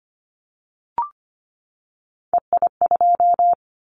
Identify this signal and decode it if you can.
Morse code, heard as a keyed tone, about 25 wpm, 700 Hz. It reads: EI2
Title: Callsign Prefixes for Top 50 DXCC - 25wpm